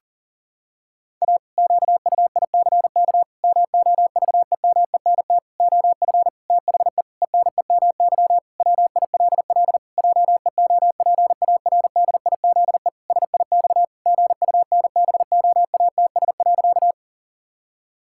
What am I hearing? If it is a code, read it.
Morse code, heard as a keyed tone, about 40 wpm, 700 Hz. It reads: A QUICK MOVEMENT OF THE ENEMY WILL JEOPARDIZE SIX GUNBOATS.
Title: Pangram Challenge v2 - 40wpm